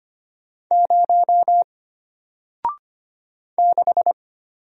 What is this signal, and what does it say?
Morse code, heard as a keyed tone, about 25 wpm, 700 Hz. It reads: 0 6
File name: Single-Letter-Number-25wpm